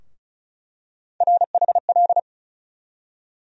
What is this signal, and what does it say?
Morse code, heard as a keyed tone, about 34 wpm, 700 Hz. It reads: RHL
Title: USA Call Sign Postfixes - 3 Letters 35wpm